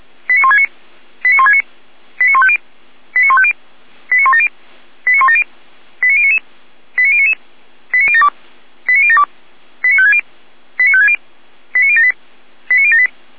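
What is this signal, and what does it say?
Five-tone selective call (ZVEI), heard as quick runs of five tones, tones of about 70 ms. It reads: 89169 89169 89150 89150 89170 89170 89099 89099 89072 89072 89560 89560 89078 89078
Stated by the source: ZVEI five-tone call received by SDR.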